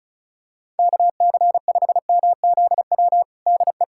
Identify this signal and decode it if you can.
Morse code, heard as a keyed tone, about 36 wpm, 700 Hz. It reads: KC5MZW DE
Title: USA Callsign DE Callsign - 6 Characters - 35wpm